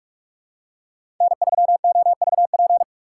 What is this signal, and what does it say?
Morse code, heard as a keyed tone, about 44 wpm, 690 Hz. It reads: N3OVP
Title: USA Call Signs - 5 Characters - 6x Character Spacing - 45wpm